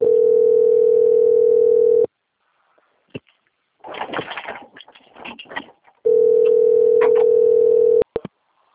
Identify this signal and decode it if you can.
Ringback tone (North America), 440 + 480 Hz.
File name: out- - - - - .7183